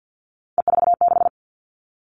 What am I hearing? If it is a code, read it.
Morse code, heard as a keyed tone, about 50 wpm, 700 Hz. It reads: E46
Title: All Cut Numbers - 3-Digit 50wpm